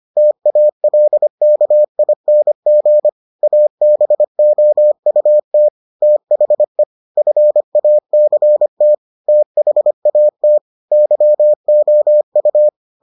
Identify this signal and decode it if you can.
Morse code, heard as a keyed tone, about 25 wpm, 600 Hz. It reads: TALKING ABOUT THE FACT THAT YOU